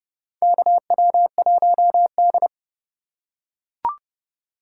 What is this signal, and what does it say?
Morse code, heard as a keyed tone, about 30 wpm, 700 Hz. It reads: KW1D
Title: USA Call Signs - 4x Character Spacing - 30wpm